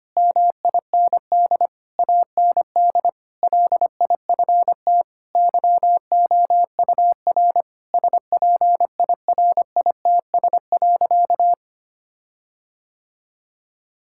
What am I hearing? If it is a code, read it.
Morse code, heard as a keyed tone, about 25 wpm, 700 Hz. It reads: MIND AND LIFT YOUR SPIRITS.
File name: practice / 25wpm / Difficult Sentences with Hints - 25wpm